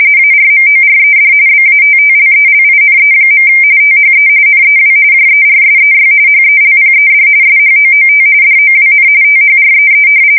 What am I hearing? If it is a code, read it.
RTTY, about 45.45 baud. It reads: ARMORED BATTLELINE. THE NEW JERSEY HELPED SHOOT DOWN THREE
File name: fsk170 / battleshipnj